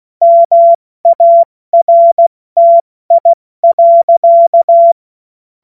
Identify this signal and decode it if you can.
Morse code, heard as a keyed tone, about 16 wpm, 680 Hz. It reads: MARTI.